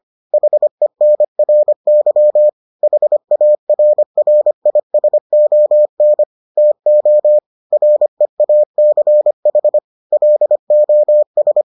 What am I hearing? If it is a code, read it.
Morse code, heard as a keyed tone, about 25 wpm, 600 Hz. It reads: HENRY HARRISON TO REACH LOS